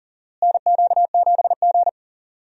Morse code, heard as a keyed tone, about 40 wpm, 700 Hz. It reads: NQ7G
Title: USA Call Signs - 4 Characters - Repeated 3x - 40wpm